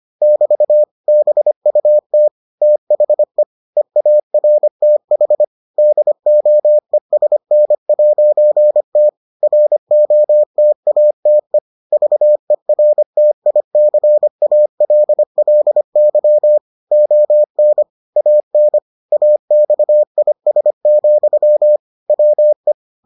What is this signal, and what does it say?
Morse code, heard as a keyed tone, about 25 wpm, 600 Hz. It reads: <BT> BUT THE EARTH DOESN'T ROTATE VERTICALLY ON AN AXIS, WE